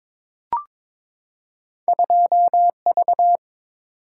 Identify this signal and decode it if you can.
Morse code, heard as a keyed tone, about 23 wpm, 700 Hz. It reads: 2V
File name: practice / 22wpm / Two Letter-Number Combinations - 22wpm